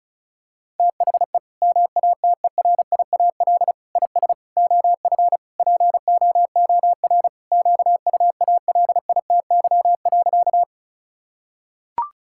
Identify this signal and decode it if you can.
Morse code, heard as a keyed tone, about 34 wpm, 700 Hz. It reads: THE MATERIAL IS OF POOR QUALITY.
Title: Sentences from Top 1000 Words - Review Part 5 - 35wpm